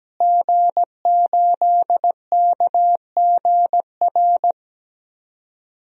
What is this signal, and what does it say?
Morse code, heard as a keyed tone, about 17 wpm, 700 Hz. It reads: G8KGR
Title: DXCC-100 Call Signs - 5 Characters - Repeated 2x - 17wpm